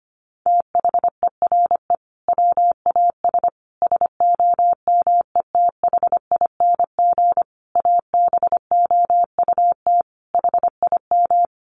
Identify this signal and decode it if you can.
Morse code, heard as a keyed tone, about 25 wpm, 700 Hz. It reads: THERE WAS SOMETHING ABOUT HIM
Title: Sentences from Top 1500 Words 25wpm